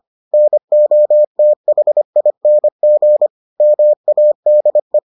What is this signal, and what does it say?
Morse code, heard as a keyed tone, about 25 wpm, 600 Hz. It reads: NOTHING MADE